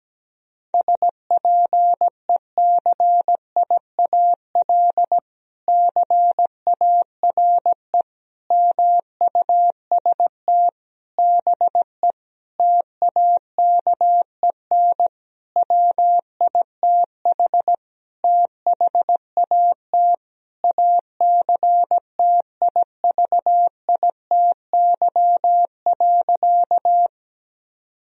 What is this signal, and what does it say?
Morse code, heard as a keyed tone, about 17 wpm, 700 Hz. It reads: SPECIAL CARE MUST BE TAKEN WITH THAT ACTIVITY.
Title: Sentences from Top 1000 Words - Review Part 4 - 17wpm